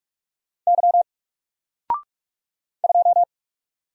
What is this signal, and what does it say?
Morse code, heard as a keyed tone, about 44 wpm, 700 Hz. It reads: Y 2
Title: Single Character (Letter:Number:Punctuation:Prosign) Alt 45wpm